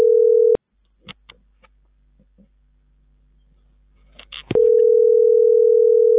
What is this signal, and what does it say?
Ringback tone (North America), 440 + 480 Hz.